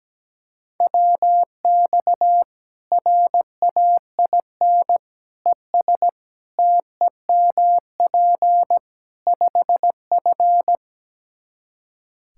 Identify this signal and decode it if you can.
Morse code, heard as a keyed tone, about 17 wpm, 700 Hz. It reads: WX RAIN ES TEMP 5F
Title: K1USN SST QSO Practice 17wpm